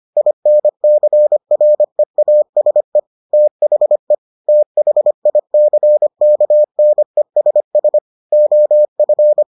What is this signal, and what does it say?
Morse code, heard as a keyed tone, about 25 wpm, 600 Hz. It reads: INCREASE THE THICKNESS OF